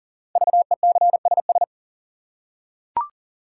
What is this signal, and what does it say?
Morse code, heard as a keyed tone, about 39 wpm, 700 Hz. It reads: VECSS
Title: Code Groups - 5 Letters - 40wpm